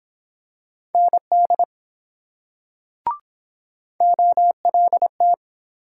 Morse code, heard as a keyed tone, about 24 wpm, 700 Hz. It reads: ND OLT